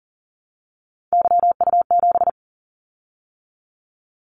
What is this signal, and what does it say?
Morse code, heard as a keyed tone, about 41 wpm, 700 Hz. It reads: YU7
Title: Callsign Prefixes for Top 25 DXCC - 40wpm